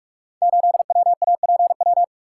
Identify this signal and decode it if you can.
Morse code, heard as a keyed tone, about 44 wpm, 700 Hz. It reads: 8WAPW
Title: Code Groups - 5 Letter:Numbers - 45wpm